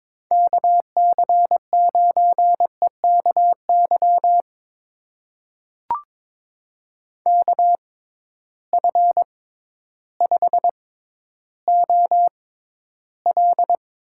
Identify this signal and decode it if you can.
Morse code, heard as a keyed tone, about 22 wpm, spaced as for 12 wpm, 700 Hz. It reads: KC9EKY KF5OL
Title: USA Call Signs - 6 Characters - 6x Character Spacing - 22wpm